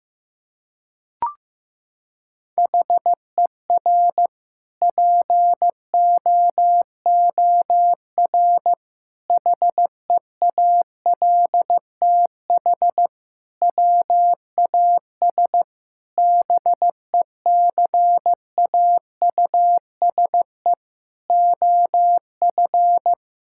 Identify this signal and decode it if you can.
Morse code, heard as a keyed tone, about 15 wpm, 700 Hz. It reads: HER POOR HEALTH WAS BECAUSE OF
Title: Sentences from Top 1000 Words - Review Part 6 - 15wpm